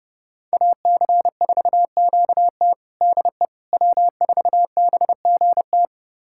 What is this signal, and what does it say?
Morse code, heard as a keyed tone, about 32 wpm, 700 Hz. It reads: AC4QT DE W4BGT
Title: USA Callsign DE Callsign - 5 Characters - 30wpm